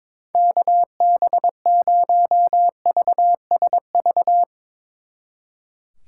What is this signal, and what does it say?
Morse code, heard as a keyed tone, about 22 wpm, 700 Hz. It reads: KB0VSV
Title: Speed Racing - USA Call Signs 22wpm